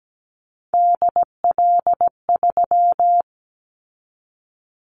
Morse code, heard as a keyed tone, about 17 wpm, 700 Hz. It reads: DL3
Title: Callsign Prefixes for Top 50 DXCC - 17wpm